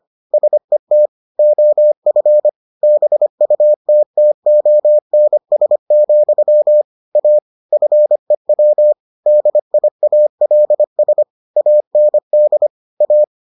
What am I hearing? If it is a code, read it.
Morse code, heard as a keyed tone, about 25 wpm, 600 Hz. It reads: SET OF BUTTONS, A FEW DIALS AND A